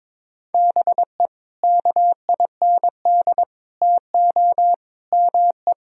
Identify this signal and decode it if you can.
Morse code, heard as a keyed tone, about 22 wpm, 700 Hz. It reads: BE KIND TO ME
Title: Sentences from Top 1000 Words - Review Part 2 - 22wpm